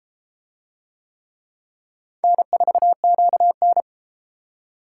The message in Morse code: N4QN

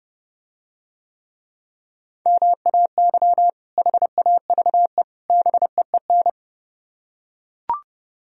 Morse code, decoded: MAY HAVE BEEN

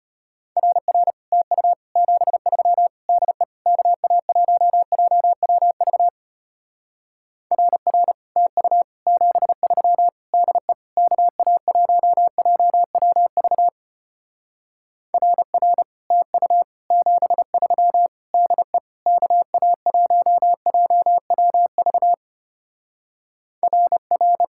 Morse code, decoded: RR TU 73 DE KA1JWV RR TU 73 DE KA1JWV RR TU 73 DE KA1JWV RR

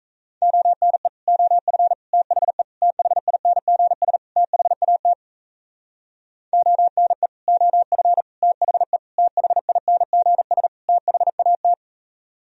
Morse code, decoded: ONE OF THE THINGS THAT ONE OF THE THINGS THAT

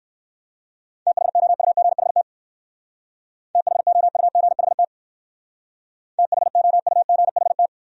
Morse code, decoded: THOUGHT THOUGHT THOUGHT

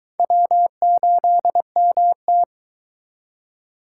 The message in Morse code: W8MT

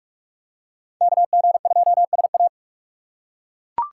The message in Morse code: KG2SA

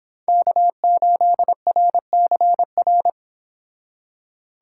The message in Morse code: K8RCR

